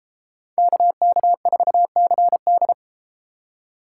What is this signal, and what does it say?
Morse code, heard as a keyed tone, about 32 wpm, 700 Hz. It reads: KK4CD